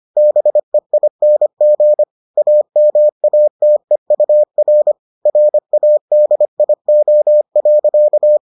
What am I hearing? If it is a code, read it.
Morse code, heard as a keyed tone, about 25 wpm, 600 Hz. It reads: BEING AMATEUR RADIO.